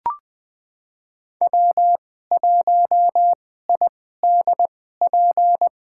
Morse code, heard as a keyed tone, about 20 wpm, spaced as for 14 wpm, 700 Hz. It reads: W1IDP
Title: USA Call Signs - 2x Character Spacing - 20wpm